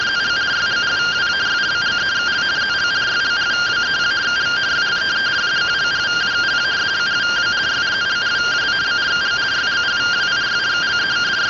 Two-tone teleprinter signal, about 170 Hz shift.